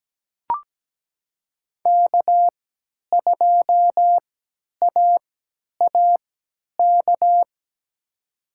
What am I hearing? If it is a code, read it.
Morse code, heard as a keyed tone, about 17 wpm, spaced as for 10 wpm, 700 Hz. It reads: K2AAK
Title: USA Call Signs - 5 Characters - 3x Character Spacing - 17wpm